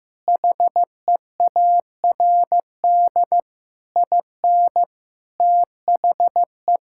Morse code, heard as a keyed tone, about 16 wpm, 700 Hz. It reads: HEARD IN THE
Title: Sentences from Top 1000 Words - Review Part 5 - 15wpm